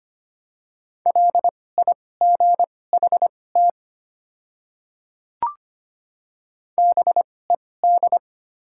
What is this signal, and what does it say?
Morse code, heard as a keyed tone, about 25 wpm, spaced as for 18 wpm, 690 Hz. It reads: LIGHT BED
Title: 2x Character Spacing - Top 500 Words - 25wpm